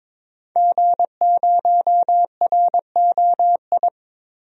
Morse code, heard as a keyed tone, about 22 wpm, 700 Hz. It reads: G0ROI
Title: DXCC-100 Call Signs - 5 Characters - 22wpm